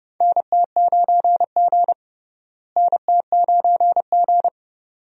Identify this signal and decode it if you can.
Morse code, heard as a keyed tone, about 30 wpm, 700 Hz. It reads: NT9G NT9G
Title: USA Call Signs - 4 Characters - Repeated 2x - 30wpm